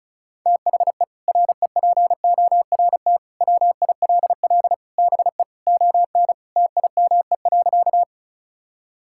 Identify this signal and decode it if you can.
Morse code, heard as a keyed tone, about 36 wpm, 700 Hz. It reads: THE REPORT WILL BE ON TIME.
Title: Sentences from Top 1000 Words - Review Part 6 - 35wpm